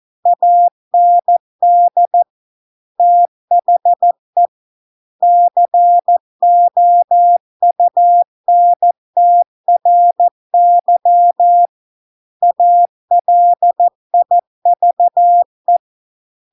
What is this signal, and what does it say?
Morse code, heard as a keyed tone, about 14 wpm, 700 Hz. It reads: AND THE COUNTRY ALIVE